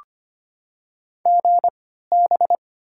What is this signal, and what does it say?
Morse code, heard as a keyed tone, about 25 wpm, spaced as for 14 wpm, 700 Hz. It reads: GB
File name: Top 90 QSO Elements - 3x Character Spacing - 25wpm